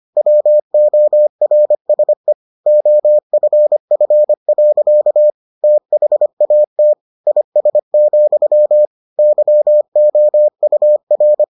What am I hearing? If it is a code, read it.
Morse code, heard as a keyed tone, about 25 wpm, 600 Hz. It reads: WORSE OFF. THAT IS, YOUR